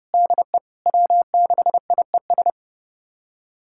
Morse code, heard as a keyed tone, about 30 wpm, 700 Hz. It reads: DE W6IES
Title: USA Callsign DE Callsign - 5 Characters - 30wpm